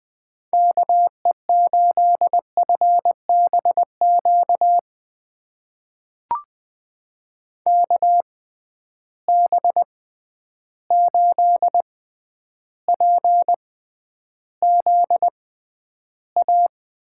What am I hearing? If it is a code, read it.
Morse code, heard as a keyed tone, about 20 wpm, spaced as for 10 wpm, 700 Hz. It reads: KE8FBQ KB8PZA
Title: USA Call Signs - 6 Characters - 6x Character Spacing - 20wpm